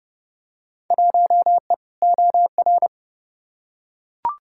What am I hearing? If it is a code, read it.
Morse code, heard as a keyed tone, about 30 wpm, 700 Hz. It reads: 1E OR
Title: Field Day Exchanges Encore - 30wpm